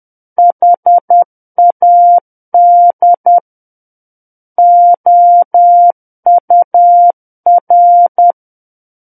Morse code, heard as a keyed tone, about 10 wpm, 700 Hz. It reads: HAD OUR
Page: War of the Worlds - 01-Chapter 1 - 10 WPM